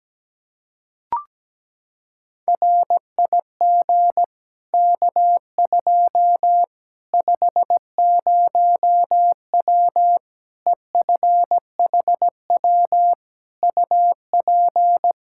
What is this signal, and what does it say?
Morse code, heard as a keyed tone, about 17 wpm, 700 Hz. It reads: RIG K2 50W EFHW UP
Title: Practice QSO - Exchange 2 - Info - 17wpm